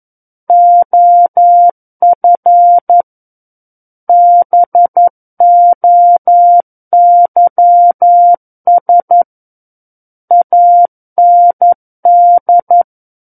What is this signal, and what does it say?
Morse code, heard as a keyed tone, about 11 wpm, 700 Hz. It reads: OF BOYS AND
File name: 11 WPM